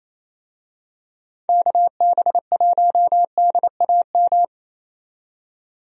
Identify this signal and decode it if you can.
Morse code, heard as a keyed tone, about 28 wpm, 700 Hz. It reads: KB1DAM
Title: USA Call Signs - 6 Characters - 28wpm